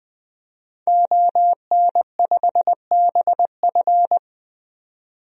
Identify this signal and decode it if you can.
Morse code, heard as a keyed tone, about 20 wpm, 700 Hz. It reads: ON5BF
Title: DXCC-100 Call Signs - 5 Characters - 20wpm